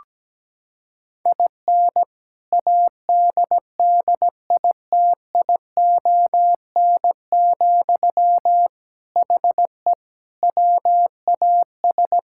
Morse code, heard as a keyed tone, about 17 wpm, 700 Hz. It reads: IN ADDITION, HE WAS
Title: Sentences from Top 1000 Words - Review Part 6 - 17wpm